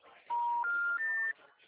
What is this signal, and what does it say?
Special information tone.
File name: out- -2011- - - .99472